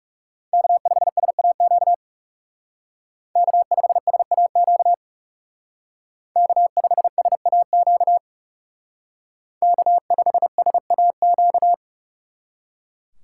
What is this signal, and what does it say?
Morse code, heard as a keyed tone, about 39 wpm, 700 Hz. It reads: K5SAQ K5SAQ K5SAQ K5SAQ